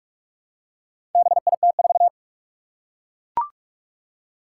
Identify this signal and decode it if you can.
Morse code, heard as a keyed tone, about 41 wpm, 700 Hz. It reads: BIT4
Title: Four Challenging Letter-Number Combinations 45wpm